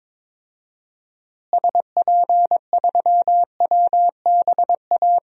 Morse code, heard as a keyed tone, about 22 wpm, 700 Hz. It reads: SP3WBA
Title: DXCC-100 Call Signs - 6 Characters - Repeated 2x - 22wpm